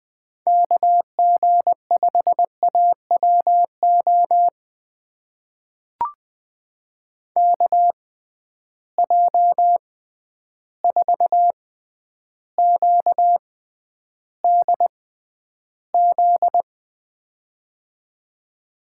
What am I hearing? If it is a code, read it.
Morse code, heard as a keyed tone, about 20 wpm, spaced as for 10 wpm, 700 Hz. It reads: KG5AWO KJ4QDZ
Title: USA Call Signs - 6 Characters - 6x Character Spacing - 20wpm